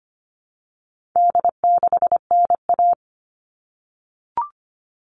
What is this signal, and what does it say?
Morse code, heard as a keyed tone, about 25 wpm, 700 Hz. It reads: D6NA
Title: All Cut Numbers - 4-Digit 25wpm